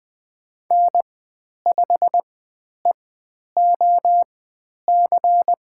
Morse code, heard as a keyed tone, about 20 wpm, spaced as for 10 wpm, 700 Hz. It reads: N5EOC
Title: 109 Lesson 18g - Letter m - Call Signs 20wpm